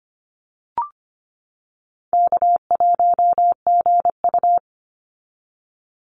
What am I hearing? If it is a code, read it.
Morse code, heard as a keyed tone, about 25 wpm, 700 Hz. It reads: K1GU
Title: Call Signs - 25wpm